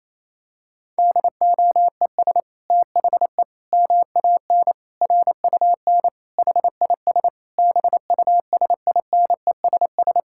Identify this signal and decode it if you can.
Morse code, heard as a keyed tone, about 28 wpm, 700 Hz. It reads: DOES THE MAN RUN HIS BUSINESS
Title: Sentences from Top 1000 Words - Review Part 5 - 28wpm